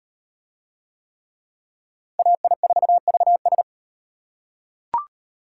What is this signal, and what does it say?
Morse code, heard as a keyed tone, about 35 wpm, 700 Hz. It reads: AI4VS